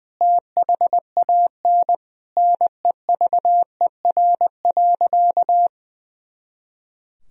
Morse code, heard as a keyed tone, about 21 wpm, 700 Hz. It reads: THAN NEVER.
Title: Sentences from Top 400 Words and Abbreviations 20wpm